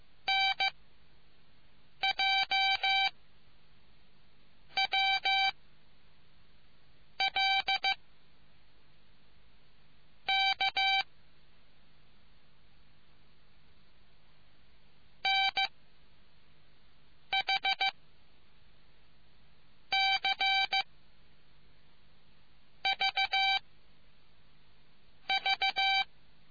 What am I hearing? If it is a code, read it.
Morse code, heard as a keyed tone, about 15 wpm, 780 Hz. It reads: N J W L K N H C V V